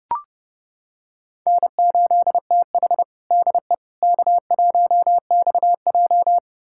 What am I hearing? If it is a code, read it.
Morse code, heard as a keyed tone, about 32 wpm, 700 Hz. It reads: N8TH DE K1XJ
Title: USA Callsign DE Callsign - 4 Characters - 30wpm